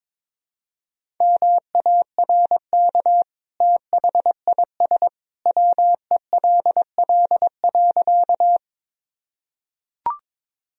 Morse code, heard as a keyed tone, about 22 wpm, 700 Hz. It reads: MARK THIS WELL.
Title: Sentences from Top 1000 Words - Review Part 1 - 22wpm